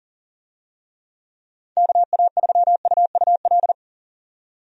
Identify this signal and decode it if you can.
Morse code, heard as a keyed tone, about 40 wpm, 700 Hz. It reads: KA3UUL
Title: USA Call Signs - 6 Characters - 40wpm